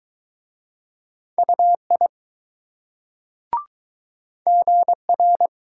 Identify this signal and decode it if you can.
Morse code, heard as a keyed tone, about 20 wpm, 700 Hz. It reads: UI GR